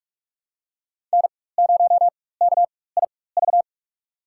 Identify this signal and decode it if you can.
Morse code, heard as a keyed tone, about 42 wpm, spaced as for 21 wpm, 700 Hz. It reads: N0KIV
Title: USA Call Signs - 5 Characters - 4x Character Spacing - 45wpm